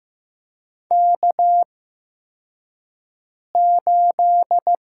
Morse code, heard as a keyed tone, about 15 wpm, spaced as for 4 wpm, 700 Hz. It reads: K8
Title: USA Call Signs - 4 Characters - 8x Character Spacing - 15wpm